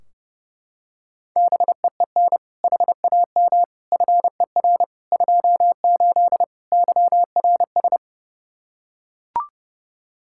Morse code, heard as a keyed tone, about 30 wpm, 700 Hz. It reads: BEEN HAM FER 28 YRS
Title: Practice QSO - Exchange 3 - More Info - 30wpm